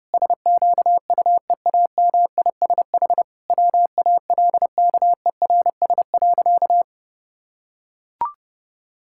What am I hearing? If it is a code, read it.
Morse code, heard as a keyed tone, about 29 wpm, 700 Hz. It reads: SQUEAMISH WALKERS.
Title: Pangram Challenge v2 - 30wpm